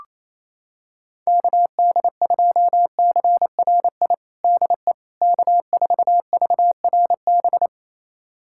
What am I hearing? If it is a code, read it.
Morse code, heard as a keyed tone, about 28 wpm, 700 Hz. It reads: KD2CRI DE K4VRB
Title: USA Callsign DE Callsign 28wpm